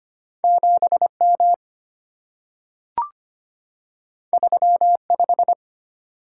Morse code, heard as a keyed tone, about 25 wpm, 690 Hz. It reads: 7M 35
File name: Two Letter-Number Combinations - 25wpm